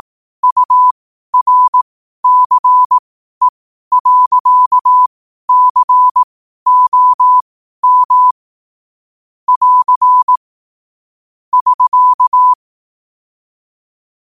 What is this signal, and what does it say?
Morse code, heard as a keyed tone, about 18 wpm, spaced as for 12 wpm, 1000 Hz. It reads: URCE.COM <AR> <SK>
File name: qotdfiles / 12wpm